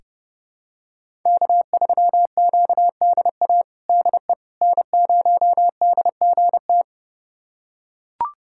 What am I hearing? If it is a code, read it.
Morse code, heard as a keyed tone, about 30 wpm, 700 Hz. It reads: K3QDA DE N0DGT
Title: USA Callsign DE Callsign - 5 Characters - 30wpm